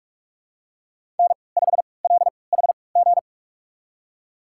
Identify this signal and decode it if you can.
Morse code, heard as a keyed tone, about 42 wpm, spaced as for 25 wpm, 700 Hz. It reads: N5LHG
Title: USA Call Signs - 3x Character Spacing - 45wpm